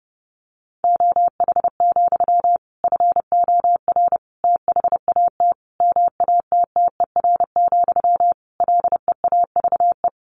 Morse code, heard as a keyed tone, about 30 wpm, 700 Hz. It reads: OH, FOR THAT MATTER, LEAVE
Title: Sentences from Top 1400 Words 30wpm